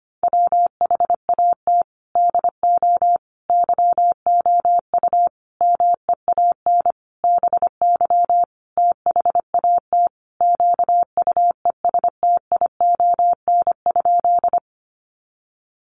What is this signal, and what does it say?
Morse code, heard as a keyed tone, about 25 wpm, 700 Hz. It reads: WHAT DO YOU MEAN BY THAT QUESTION?
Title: Sentences from Top 400 Words 25wpm